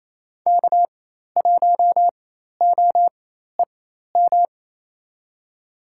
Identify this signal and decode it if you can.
Morse code, heard as a keyed tone, about 28 wpm, spaced as for 13 wpm, 700 Hz. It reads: K1OEM
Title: USA Call Signs - 4x Character Spacing - 28wpm